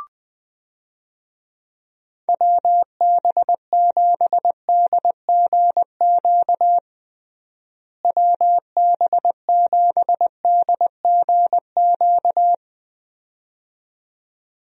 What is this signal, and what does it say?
Morse code, heard as a keyed tone, about 20 wpm, 700 Hz. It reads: WB7DGQ WB7DGQ
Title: USA Call Signs - 6 Characters - Repeated 2x - 20wpm